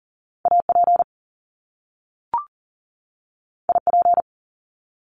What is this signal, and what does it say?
Morse code, heard as a keyed tone, about 42 wpm, 700 Hz. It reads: AP IP
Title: Top Two Letter Combinations in Words 40wpm